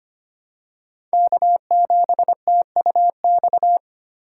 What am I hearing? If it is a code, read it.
Morse code, heard as a keyed tone, about 25 wpm, 700 Hz. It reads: K7TUX
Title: USA Call Signs - 5 Characters - 3x Character Spacing - 25wpm